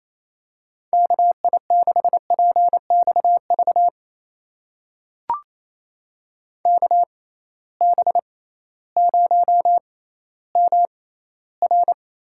Morse code, heard as a keyed tone, about 28 wpm, spaced as for 15 wpm, 700 Hz. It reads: KI6PXV KB0MR
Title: USA Call Signs - 6 Characters - 6x Character Spacing - 28wpm